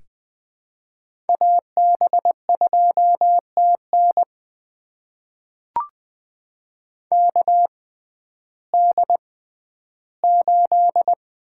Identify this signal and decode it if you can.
Morse code, heard as a keyed tone, about 20 wpm, spaced as for 12 wpm, 700 Hz. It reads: AB2TN KD8
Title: USA Call Signs - 6x Character Spacing - 20wpm